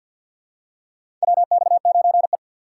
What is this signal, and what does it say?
Morse code, heard as a keyed tone, about 49 wpm, 700 Hz. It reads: WX9E
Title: USA Call Signs - 4 Characters - 3x Character Spacing - 50wpm